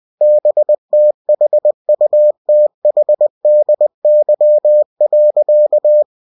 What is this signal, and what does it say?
Morse code, heard as a keyed tone, about 20 wpm, 600 Hz. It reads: BTHUTHDY.
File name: cw-books / winnie-the-pooh / winnie-20wpm-page-0052